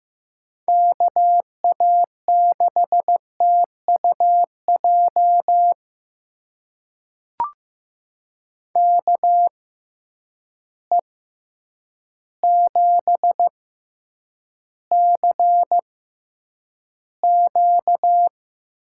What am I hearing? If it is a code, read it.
Morse code, heard as a keyed tone, about 15 wpm, spaced as for 8 wpm, 700 Hz. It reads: KA6TUJ KE7CQ
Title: USA Call Signs - 6x Character Spacing - 15wpm